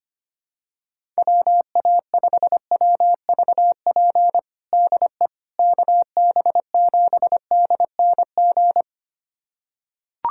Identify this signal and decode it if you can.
Morse code, heard as a keyed tone, about 25 wpm, 700 Hz. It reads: WA5WVP DE KB7DNG
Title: USA Callsign DE Callsign - 6 Characters - 25wpm